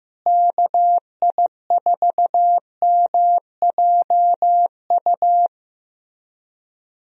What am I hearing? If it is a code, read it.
Morse code, heard as a keyed tone, about 15 wpm, 700 Hz. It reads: KI4MJU
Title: USA Call Signs - 6 Characters - Repeated 3x - 15wpm